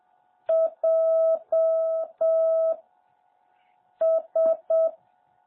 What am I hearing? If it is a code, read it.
Morse code, heard as a keyed tone, about 7 wpm, 650 Hz. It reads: J S
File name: jh8cbh / 2021cw